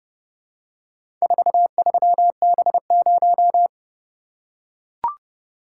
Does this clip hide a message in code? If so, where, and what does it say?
Yes: Morse code from 1.2 to 3.7 s, heard as a keyed tone, about 31 wpm, 700 Hz. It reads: 43B0